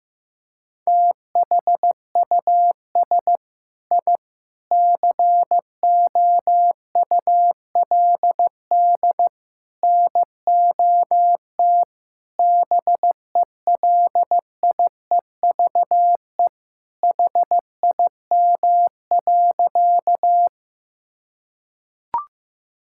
Morse code, heard as a keyed tone, about 15 wpm, 700 Hz. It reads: THUS I COULD NOT BELIEVE HIM.
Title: Sentences from Top 1000 Words - Review Part 5 - 15wpm